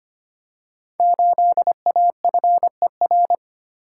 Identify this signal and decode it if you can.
Morse code, heard as a keyed tone, about 25 wpm, 700 Hz. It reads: 8AFER
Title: Code Groups - 5 Letter:Numbers - 25wpm